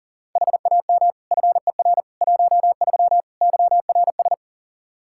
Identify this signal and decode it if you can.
Morse code, heard as a keyed tone, about 37 wpm, 700 Hz. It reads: HAM FER 13 YRS
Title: K1USN SST QSO Practice 40wpm